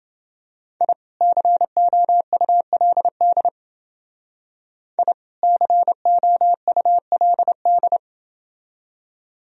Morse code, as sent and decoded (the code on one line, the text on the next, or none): .. / -.-. --- ..- .-.. -.. / .. / -.-. --- ..- .-.. -..
I COULD I COULD